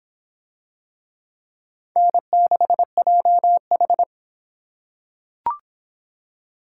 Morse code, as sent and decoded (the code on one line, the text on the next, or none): -. -.... .--- ....
N6JH